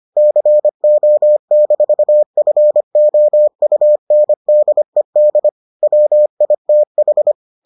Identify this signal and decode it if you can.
Morse code, heard as a keyed tone, about 25 wpm, 600 Hz. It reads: CO-FOUNDED WITH